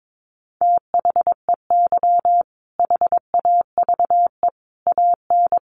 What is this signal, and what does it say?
Morse code, heard as a keyed tone, about 22 wpm, 700 Hz. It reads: THEY HAVE AN
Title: Sentences from Top 1500 Words 22wpm